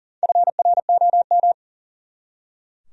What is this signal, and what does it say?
Morse code, heard as a keyed tone, about 40 wpm, 700 Hz. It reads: FROM
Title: Top 100 Words 40wpm